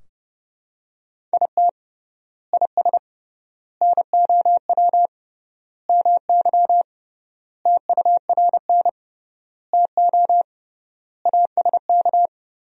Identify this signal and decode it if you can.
Morse code, heard as a keyed tone, about 30 wpm, 700 Hz. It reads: IT IS NOW MY TURN TO ASK
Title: 3x Word-Spacing - Sentences from Top 200 Words - 30wpm